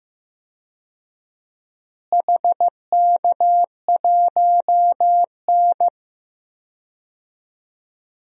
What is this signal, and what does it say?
Morse code, heard as a keyed tone, about 15 wpm, 700 Hz. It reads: HK1N
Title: DXCC-100 Call Signs - 4 Characters - Repeated 3x - 15wpm